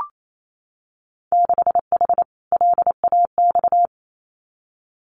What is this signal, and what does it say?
Morse code, heard as a keyed tone, about 28 wpm, 700 Hz. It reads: 6H LAX
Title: Winter Field Day Exchanges Encore 28wpm